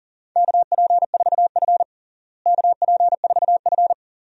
Morse code, heard as a keyed tone, about 40 wpm, 700 Hz. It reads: KP4F KP4F
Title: DXCC-100 Call Signs - 4 Characters - Repeated 2x - 40wpm